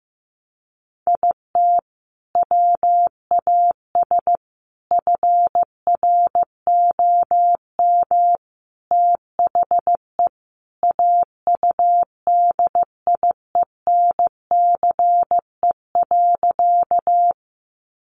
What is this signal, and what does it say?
Morse code, heard as a keyed tone, about 15 wpm, 700 Hz. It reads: IT WAS FROM THE AUDIENCE.
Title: Sentences from Top 800 Words 15wpm